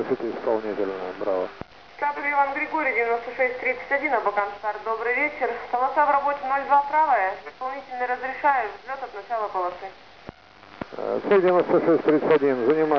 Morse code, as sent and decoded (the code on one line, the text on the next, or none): none